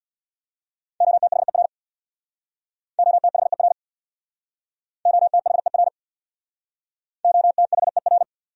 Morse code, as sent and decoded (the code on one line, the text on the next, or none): --- - .... .-. / --- - .... .-. / --- - .... .-. / --- - .... .-.
OTHER OTHER OTHER OTHER